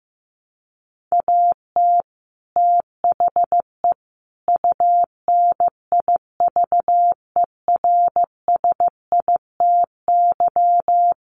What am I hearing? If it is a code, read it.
Morse code, heard as a keyed tone, about 15 wpm, 700 Hz. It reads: AT THE UNIVERSITY
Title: Sets of 4 Words 15wpm